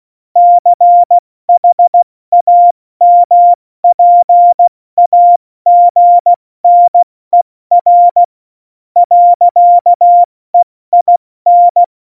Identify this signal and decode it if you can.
Morse code, heard as a keyed tone, about 16 wpm, spaced as for 13 wpm, 700 Hz. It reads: CHAMPAGNER. EIN
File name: Morsekurs / Witze / Witz 17. November 2020, 70 F80